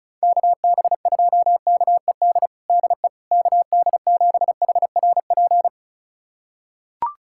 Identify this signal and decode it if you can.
Morse code, heard as a keyed tone, about 36 wpm, 700 Hz. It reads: KB2KED DE KD7HRP
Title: USA Callsign DE Callsign 35wpm